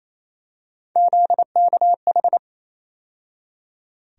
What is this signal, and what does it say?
Morse code, heard as a keyed tone, about 28 wpm, 700 Hz. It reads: ZKH
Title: USA Call Sign Postfixes - 3 Letters 28wpm